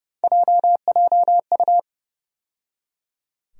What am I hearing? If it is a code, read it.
Morse code, heard as a keyed tone, about 30 wpm, 700 Hz. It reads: JJU
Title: USA Call Sign Postfixes - 3 Letters 30wpm